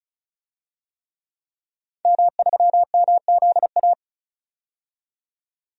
Morse code, heard as a keyed tone, about 36 wpm, 700 Hz. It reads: M3MZA
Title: DXCC-100 Call Signs - 5 Characters - 35wpm